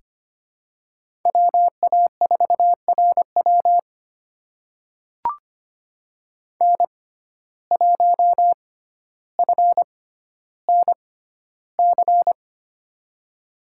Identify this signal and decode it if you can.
Morse code, heard as a keyed tone, about 25 wpm, spaced as for 13 wpm, 700 Hz. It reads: WA4RW N1FNC
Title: USA Call Signs - 5 Characters - 6x Character Spacing - 25wpm